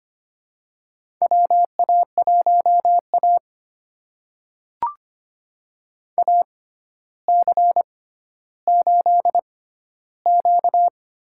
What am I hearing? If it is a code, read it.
Morse code, heard as a keyed tone, about 25 wpm, spaced as for 13 wpm, 700 Hz. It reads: WA1A AC8Q
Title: USA Call Signs - 4 Characters - 6x Character Spacing - 25wpm